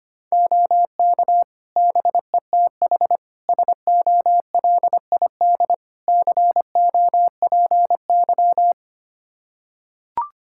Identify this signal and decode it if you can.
Morse code, heard as a keyed tone, about 25 wpm, 700 Hz. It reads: OK BETH SOLID COPY
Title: Practice QSO - Exchange 3 - More Info - 25wpm